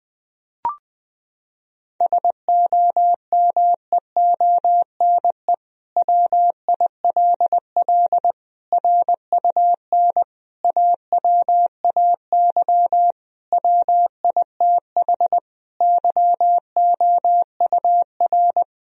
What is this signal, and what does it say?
Morse code, heard as a keyed tone, about 21 wpm, 700 Hz. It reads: SOMEONE WILL RUN AWAY WITH YOUR
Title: Sentences from Top 1000 Words - Review Part 1 - 20wpm